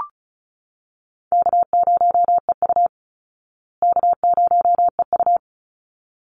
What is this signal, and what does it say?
Morse code, heard as a keyed tone, about 34 wpm, 700 Hz. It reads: K0EU K0EU
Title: Call Signs Repeated 3x - 35wpm